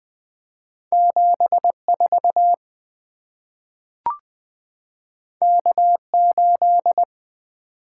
Morse code, heard as a keyed tone, about 20 wpm, 700 Hz. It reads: 7 4 K 8
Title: Two Letter-Number Combinations - 20wpm